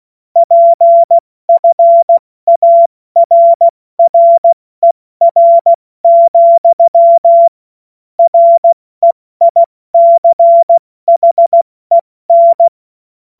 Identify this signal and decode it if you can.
Morse code, heard as a keyed tone, about 16 wpm, 680 Hz. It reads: PFARRER, REICHEN
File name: Morsekurs / Witze / Witz 17.Oktober 2023 70 F80